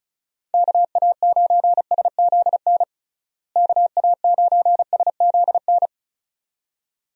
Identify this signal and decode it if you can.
Morse code, heard as a keyed tone, about 34 wpm, 700 Hz. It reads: KA9SZN KA9SZN
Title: USA Call Signs - 6 Characters - Repeated 2x - 35wpm